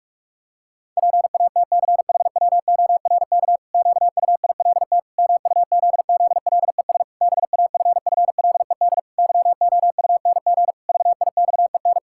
Morse code, heard as a keyed tone, about 46 wpm, 700 Hz. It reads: PATCHWORK QUILT MUZZLES BAFFLED YOUNG VIXEN
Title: Pangram Challenge v2 - 45wpm